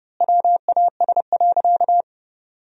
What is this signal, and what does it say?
Morse code, heard as a keyed tone, about 30 wpm, 700 Hz. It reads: WAS.
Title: 2x Word-Spacing - Sentences from Top 100 Words - 30wpm